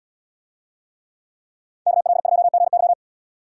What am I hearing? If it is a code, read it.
Morse code, heard as a keyed tone, about 77 wpm, 700 Hz. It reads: KL2GY